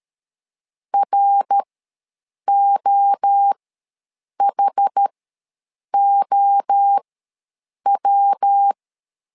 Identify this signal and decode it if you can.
Morse code, heard as a keyed tone, about 13 wpm, spaced as for 7 wpm, 800 Hz. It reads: ROHOW
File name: LEZIONI / 3 MWRO / 3 MWRO ws 12wpm 1914